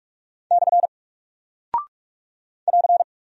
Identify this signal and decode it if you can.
Morse code, heard as a keyed tone, about 43 wpm, 700 Hz. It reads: / <AR>
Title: Single Character (Letter:Number:Punctuation:Prosign) Alt 45wpm